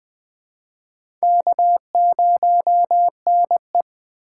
Morse code, heard as a keyed tone, about 20 wpm, 700 Hz. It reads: K0NE
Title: USA Call Signs - 4 Characters - 4x Character Spacing - 20wpm